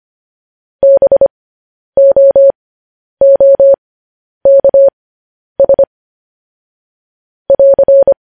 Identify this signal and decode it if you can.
Morse code, heard as a keyed tone, about 25 wpm, spaced as for 10 wpm, 550 Hz. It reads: BOOKS <AR>
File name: podcast / News headlines-1x-25x10